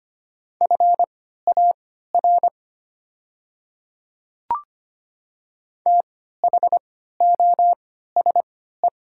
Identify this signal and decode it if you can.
Morse code, heard as a keyed tone, about 25 wpm, spaced as for 14 wpm, 700 Hz. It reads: FAR THOSE
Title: 3x Character Spacing - Top 300 Words - 25wpm